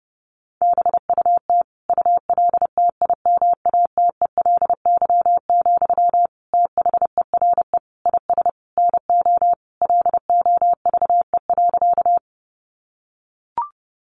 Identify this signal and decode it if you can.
Morse code, heard as a keyed tone, about 28 wpm, 700 Hz. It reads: BUT ULTIMATELY, THERE IS NO LOVE.
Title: Sentences from Top 1200 Words 30wpm